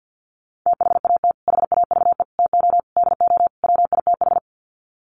Morse code, heard as a keyed tone, about 50 wpm, 700 Hz. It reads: THAT HAVE TO DO WITH
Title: Sets of 5 Words Encore2 50wpm